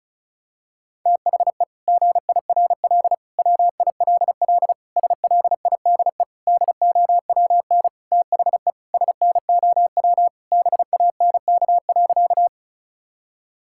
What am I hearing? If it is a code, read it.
Morse code, heard as a keyed tone, about 36 wpm, 700 Hz. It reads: THE GIRL WILL SLIDE DOWN THE SNOW BANK.
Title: Sentences from Top 1600 Words 35wpm